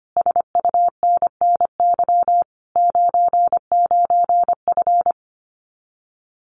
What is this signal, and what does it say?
Morse code, heard as a keyed tone, about 25 wpm, 700 Hz. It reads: SUNNY 99F
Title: Common QSO Phrases Encore 25wpm